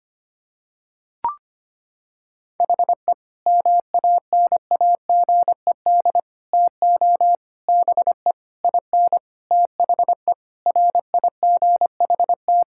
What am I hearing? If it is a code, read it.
Morse code, heard as a keyed tone, about 25 wpm, 700 Hz. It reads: HE MANAGED TO BE IN THE RIGHT
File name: practice / 25wpm / Sentences with 9-Word Phrases - 25wpm